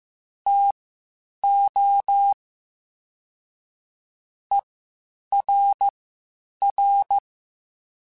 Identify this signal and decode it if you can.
Morse code, heard as a keyed tone, about 15 wpm, spaced as for 8 wpm, 800 Hz. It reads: TO ERR
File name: Spaced / Chuckles Spaced-mp3s / HumanErr sp @13